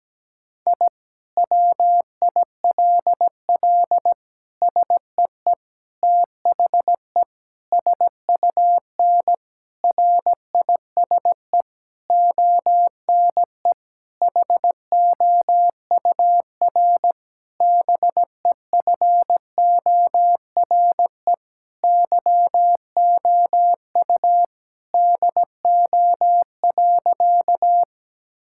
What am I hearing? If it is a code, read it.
Morse code, heard as a keyed tone, about 17 wpm, 700 Hz. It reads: I WILL SEE THE SUN RISE ONE HOUR BEFORE YOU DO.
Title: Sentences from Top 1000 Words - Review Part 6 - 17wpm